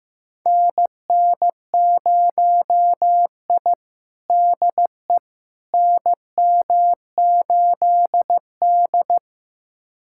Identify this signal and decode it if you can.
Morse code, heard as a keyed tone, about 15 wpm, 700 Hz. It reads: NN0I DE NM8D
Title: USA Callsign DE Callsign - 4 Characters - 15wpm